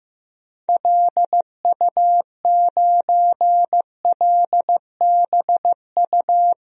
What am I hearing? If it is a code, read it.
Morse code, heard as a keyed tone, about 15 wpm, 700 Hz. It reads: LU9LBU
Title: DXCC-100 Call Signs - 6 Characters - Repeated 2x - 15wpm